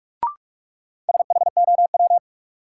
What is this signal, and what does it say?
Morse code, heard as a keyed tone, about 44 wpm, 700 Hz. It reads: SHOW
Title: Sets of 5 Words - New Word Drill - 45wpm